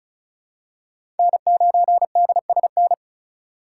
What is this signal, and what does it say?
Morse code, heard as a keyed tone, about 36 wpm, 700 Hz. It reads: N9DSN
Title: USA Call Signs - 5 Characters - 3x Character Spacing - 35wpm